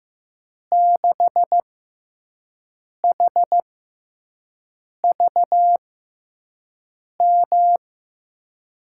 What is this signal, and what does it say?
Morse code, heard as a keyed tone, about 15 wpm, spaced as for 5 wpm, 700 Hz. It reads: 6HVM
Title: USA Call Signs - 6 Characters - 6x Character Spacing - 15wpm